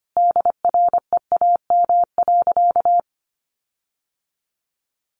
Morse code, heard as a keyed tone, about 25 wpm, 700 Hz. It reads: DREAM.
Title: Sentences from Top 1500 Words 25wpm